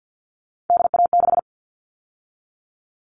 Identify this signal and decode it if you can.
Morse code, heard as a keyed tone, about 50 wpm, 700 Hz. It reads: DA6
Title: All Cut Numbers - 3-Digit 50wpm